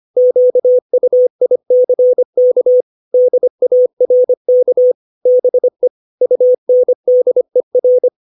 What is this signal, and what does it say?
Morse code, heard as a keyed tone, about 25 wpm, 500 Hz. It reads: QUICK DARK BE UNDER
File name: 632 words / 632 Words 25WPM